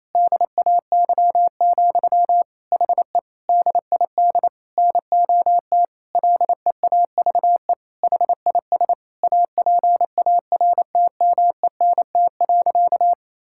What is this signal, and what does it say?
Morse code, heard as a keyed tone, about 28 wpm, 700 Hz. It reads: DAY, HE DID NOT LEAVE HIS APARTMENT.
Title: Sentences from Top 1000 Words - Review Part 6 - 28wpm